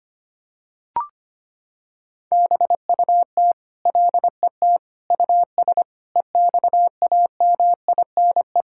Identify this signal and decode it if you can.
Morse code, heard as a keyed tone, about 25 wpm, 700 Hz. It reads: BUT LET US EXAMINE
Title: Sentences from Top 1600 Words 25wpm